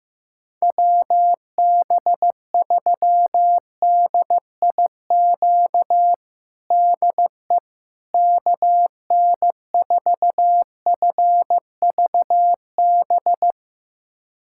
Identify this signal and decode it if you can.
Morse code, heard as a keyed tone, about 16 wpm, 700 Hz. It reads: WB3DIQ DE KN4FVB
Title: USA Callsign DE Callsign - 6 Characters - 15wpm